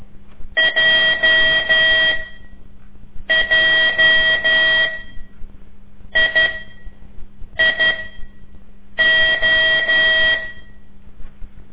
Morse code, heard as a keyed tone, about 12 wpm, spaced as for 6 wpm, 1950 Hz. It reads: JJIIO
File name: ik1weg2 / CORSI CW / lez5-2